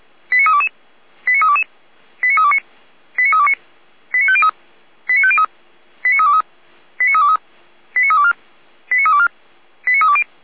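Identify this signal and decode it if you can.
Five-tone selective call (ZVEI), heard as quick runs of five tones, tones of about 70 ms. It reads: 89322 89322 89329 89329 89503 89503 89323 89323 89324 89324 89320
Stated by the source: ZVEI five-tone call 89203 received by SDR.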